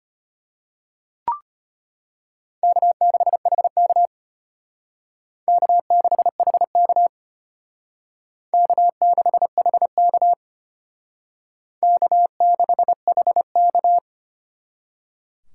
Morse code, heard as a keyed tone, about 32 wpm, 700 Hz. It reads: K6HK K6HK K6HK K6HK